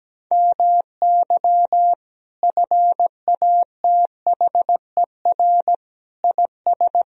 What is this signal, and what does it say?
Morse code, heard as a keyed tone, about 17 wpm, 700 Hz. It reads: MY FATHER IS
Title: Sentences from Top 1000 Words - Review Part 1 - 17wpm